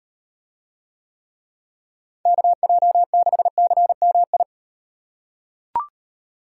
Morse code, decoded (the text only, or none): KJ6CMI